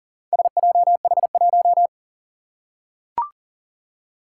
Morse code, heard as a keyed tone, about 40 wpm, 700 Hz. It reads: SJH1
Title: Four Challenging Letter-Number Combinations 40wpm